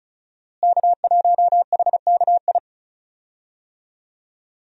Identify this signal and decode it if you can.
Morse code, heard as a keyed tone, about 34 wpm, 700 Hz. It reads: K1HKI
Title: USA Call Signs - 5 Characters - Repeated 2x - 35wpm